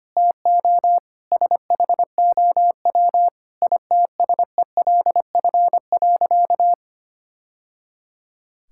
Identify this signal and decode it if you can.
Morse code, heard as a keyed tone, about 25 wpm, 700 Hz. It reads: TO SHOW ITSELF.
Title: Sentences from Top 1000 Words - Review Part 5 - 25wpm